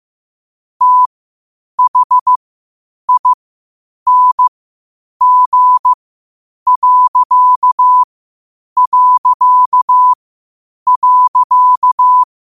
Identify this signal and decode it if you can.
Morse code, heard as a keyed tone, about 15 wpm, spaced as for 8 wpm, 1000 Hz. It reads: THING...
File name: qotdfiles / 8wpm